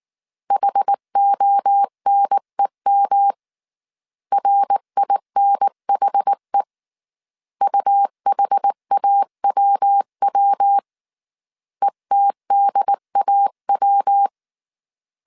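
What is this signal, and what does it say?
Morse code, heard as a keyed tone, about 19 wpm, 800 Hz. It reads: HONEM RINHE UHAWW ETDAW
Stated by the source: Morse code dictation at 18 wpm from the IW3SGT QRS Morse code training course, five-character groups of 300 characters.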